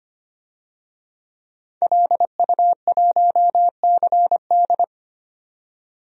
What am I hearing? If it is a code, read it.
Morse code, heard as a keyed tone, about 25 wpm, 700 Hz. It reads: LU1CD
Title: DXCC-100 Call Signs - 5 Characters - 25wpm